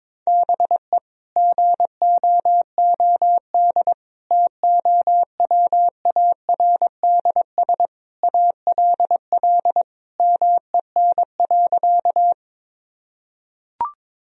Morse code, heard as a keyed tone, about 22 wpm, 700 Hz. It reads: BE GOOD TOWARDS ALL MEN.
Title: Sentences from Top 1000 Words - Review Part 2 - 22wpm